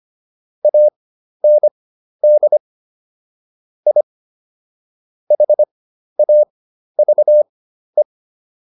Morse code, heard as a keyed tone, about 25 wpm, spaced as for 12 wpm, 600 Hz. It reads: AND I HAVE